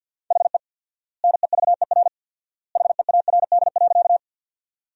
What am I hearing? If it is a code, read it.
Morse code, heard as a keyed tone, about 50 wpm, 700 Hz. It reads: HE NEVER HEARD.
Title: 4x Word-Spacing - Sentences from Top 400 Words - 50wpm